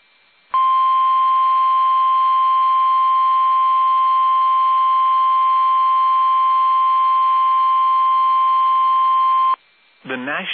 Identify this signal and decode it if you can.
Emergency Alert System (SAME) tones (the 1050 Hz attention tone), with speech. It is heard from 0.5 to 9.6 s.